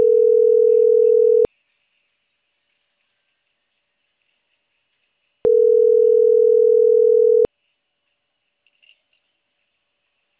Ringback tone (North America), 440 + 480 Hz.
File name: CENTRAL CAEA / out agcaea01